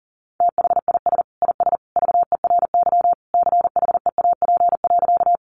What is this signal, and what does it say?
Morse code, heard as a keyed tone, about 40 wpm, 700 Hz. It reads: THIS IS VERY CHEAP.
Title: Sentences from Top 1200 Words 40wpm